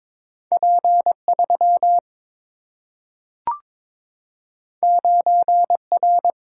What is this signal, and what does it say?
Morse code, heard as a keyed tone, about 22 wpm, 700 Hz. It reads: P3 9R
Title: Single-Letter Single-Number 22wpm